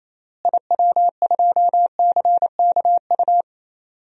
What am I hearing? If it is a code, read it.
Morse code, heard as a keyed tone, about 28 wpm, 700 Hz. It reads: IW2CKU
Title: DXCC-100 Call Signs - 6 Characters - Repeated 2x - 28wpm